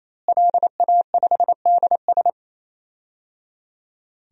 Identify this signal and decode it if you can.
Morse code, heard as a keyed tone, about 27 wpm, 700 Hz. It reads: LA5DS
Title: DXCC-100 Call Signs - 5 Characters - Repeated 3x - 28wpm